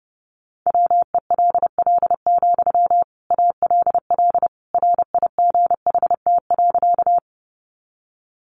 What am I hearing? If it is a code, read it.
Morse code, heard as a keyed tone, about 30 wpm, 700 Hz. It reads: WELL, ALL RIGHT.
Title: Sentences from Top 200 Words 30wpm